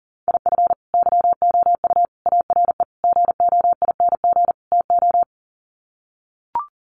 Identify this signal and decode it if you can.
Morse code, heard as a keyed tone, about 40 wpm, 700 Hz. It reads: IF YOU ARE GOING TO
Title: Sets of 5 Words Encore2 40wpm